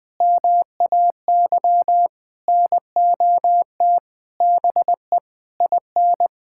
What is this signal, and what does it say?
Morse code, heard as a keyed tone, about 20 wpm, 700 Hz. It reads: MAY NOT BE IN
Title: Sentences with 7-Word Phrases - Spoken Beginning - 20wpm